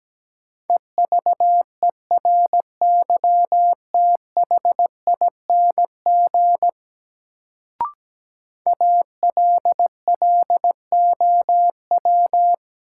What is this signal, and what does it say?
Morse code, heard as a keyed tone, about 17 wpm, 700 Hz. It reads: EVERYTHING ALLOW
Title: Sentences from Top 700 Words - New Word Drill - 17wpm